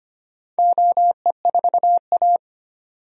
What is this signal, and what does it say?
Morse code, heard as a keyed tone, about 25 wpm, 690 Hz. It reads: OE4A
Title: DXCC-100 Call Signs - 4 Characters - 25wpm